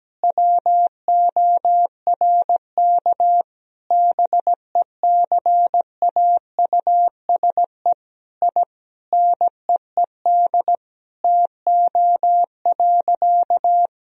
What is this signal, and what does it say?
Morse code, heard as a keyed tone, about 17 wpm, 700 Hz. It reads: WORK BECAUSE I NEED TO.
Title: Sentences from Top 1000 Words - Review Part 3 - 17wpm